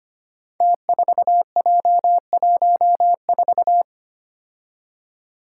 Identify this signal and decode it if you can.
Morse code, heard as a keyed tone, about 25 wpm, 700 Hz. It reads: T4J14
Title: Five Challenging Letter-Number Combinations 25wpm